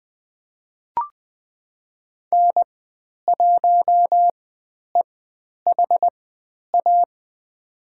Morse code, heard as a keyed tone, about 20 wpm, spaced as for 10 wpm, 700 Hz. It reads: N1EHA
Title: 109 Lesson 18g - Letter m - Call Signs 20wpm